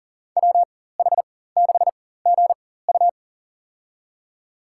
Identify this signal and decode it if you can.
Morse code, heard as a keyed tone, about 40 wpm, spaced as for 19 wpm, 700 Hz. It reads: WH6GU
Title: USA Call Signs - 5 Characters - 4x Character Spacing - 40wpm